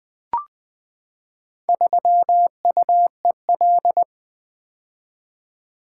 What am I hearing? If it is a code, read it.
Morse code, heard as a keyed tone, about 20 wpm, 700 Hz. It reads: 3UEL
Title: Four Letter-Number Combinations - 20wpm